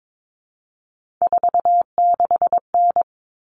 Morse code, heard as a keyed tone, about 22 wpm, 700 Hz. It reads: 46N
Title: All Cut Numbers - 3-Digit 22wpm